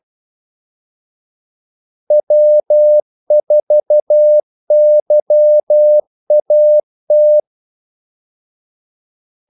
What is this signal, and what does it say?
Morse code, heard as a keyed tone, about 12 wpm, 600 Hz. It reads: W4YAT
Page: 2 ekstra ordmellemrum mellem kaldesignalerne